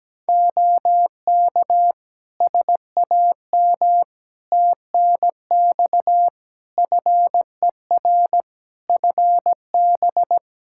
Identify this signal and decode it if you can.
Morse code, heard as a keyed tone, about 17 wpm, 700 Hz. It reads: OK SAM TNX FER FB
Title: Practice QSO - Exchange 4 - Ending - 17wpm